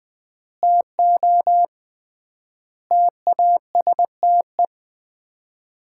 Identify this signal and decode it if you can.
Morse code, heard as a keyed tone, about 19 wpm, 700 Hz. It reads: TO TASTE
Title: 3x Word-Spacing - Sets of 5 Words - 20wpm